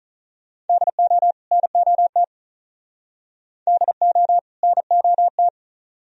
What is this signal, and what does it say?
Morse code, heard as a keyed tone, about 45 wpm, 700 Hz. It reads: DO NOT DO NOT